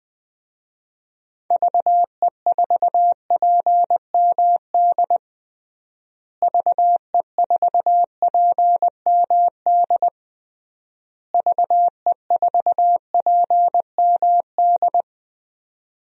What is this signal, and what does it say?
Morse code, heard as a keyed tone, about 20 wpm, 700 Hz. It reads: VE4PMD VE4PMD VE4PMD
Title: DXCC-100 Call Signs - 6 Characters - Repeated 3x - 20wpm